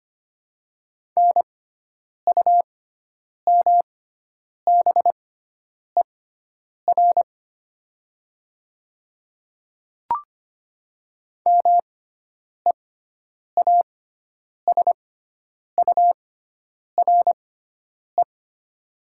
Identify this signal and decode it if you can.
Morse code, heard as a keyed tone, about 25 wpm, spaced as for 9 wpm, 700 Hz. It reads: NUMBER MEASURE
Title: 6x Character Spacing - Top 500 Words - 25wpm